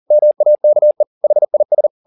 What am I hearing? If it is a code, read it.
Morse code, heard as a keyed tone, about 40 wpm, 600 Hz. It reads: MAKE HIS
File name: the old man and the sea 2 40wpm